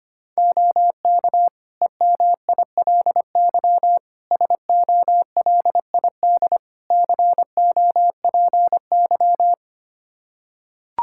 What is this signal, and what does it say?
Morse code, heard as a keyed tone, about 25 wpm, 700 Hz. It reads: OK EMILY SOLID COPY
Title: Practice QSO - Exchange 3 - More Info - 25wpm